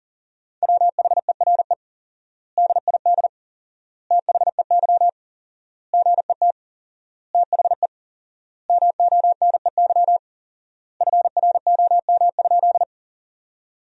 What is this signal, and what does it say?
Morse code, heard as a keyed tone, about 40 wpm, 700 Hz. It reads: WHERE DID THEY GET THE MONEY FROM?
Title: 4x Word-Spacing - Sentences from Top 400 Words - 40wpm